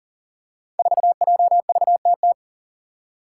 Morse code, heard as a keyed tone, about 40 wpm, 700 Hz. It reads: 4JVTT
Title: Five Challenging Letter-Number Combinations 40wpm